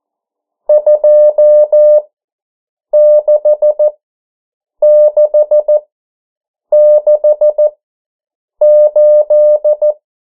Morse code, heard as a keyed tone, about 14 wpm, spaced as for 7 wpm, 600 Hz. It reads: 26668
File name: morse mp3 / 09 urok 8